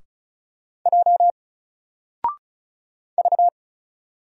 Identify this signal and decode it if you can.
Morse code, heard as a keyed tone, about 36 wpm, 700 Hz. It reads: J V
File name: Single-Letter-Number-35wpm